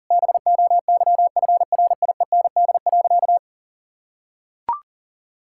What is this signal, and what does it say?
Morse code, heard as a keyed tone, about 37 wpm, 700 Hz. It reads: BOYFRIEND.
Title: Sentences from Top 1000 Words - Review Part 5 - 40wpm